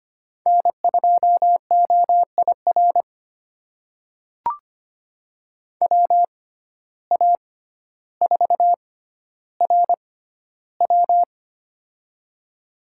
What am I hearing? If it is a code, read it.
Morse code, heard as a keyed tone, about 25 wpm, spaced as for 13 wpm, 700 Hz. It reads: N2OIR WA4RW
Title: USA Call Signs - 5 Characters - 6x Character Spacing - 25wpm